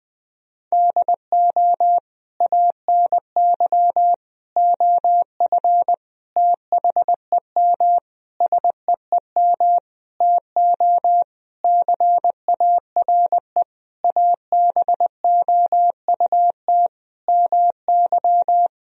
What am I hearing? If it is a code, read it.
Morse code, heard as a keyed tone, about 21 wpm, 700 Hz. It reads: DO ANY OF THEM SEEM TO CARE ABOUT MY
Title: Sentences from Top 1000 Words - Review Part 3 - 20wpm